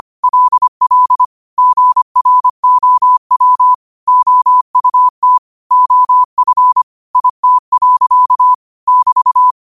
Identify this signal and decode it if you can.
Morse code, heard as a keyed tone, about 25 wpm, 1000 Hz. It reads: LL GROW OUT OF IT. <BT>
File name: qotdfiles / 25wpm